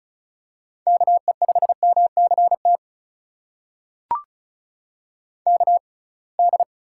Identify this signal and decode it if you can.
Morse code, heard as a keyed tone, about 36 wpm, spaced as for 27 wpm, 700 Hz. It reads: KE5MCT KD
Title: USA Call Signs - 6x Character Spacing - 35wpm